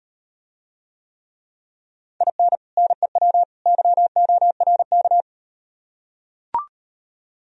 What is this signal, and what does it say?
Morse code, heard as a keyed tone, about 38 wpm, 700 Hz. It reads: IN NEW YORK